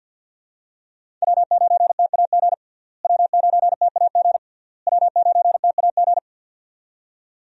Morse code, heard as a keyed tone, about 50 wpm, 700 Hz. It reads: W9TAG W9TAG W9TAG
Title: USA Call Signs - 5 Characters - Repeated 3x - 50wpm